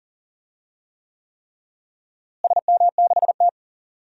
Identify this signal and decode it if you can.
Morse code, heard as a keyed tone, about 40 wpm, 700 Hz. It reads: SM6T
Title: DXCC-100 Call Signs - 4 Characters - Repeated 3x - 40wpm